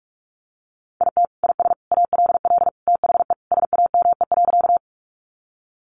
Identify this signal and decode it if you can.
Morse code, heard as a keyed tone, about 44 wpm, 700 Hz. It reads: IT IS ALL THE SAME.
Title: Sentences from Top 200 Words 45wpm